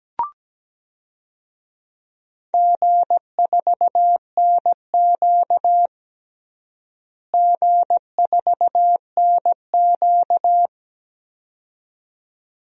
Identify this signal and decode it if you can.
Morse code, heard as a keyed tone, about 17 wpm, 700 Hz. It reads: G4NQ G4NQ
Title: DXCC-100 Call Signs - 4 Characters - Repeated 2x - 17wpm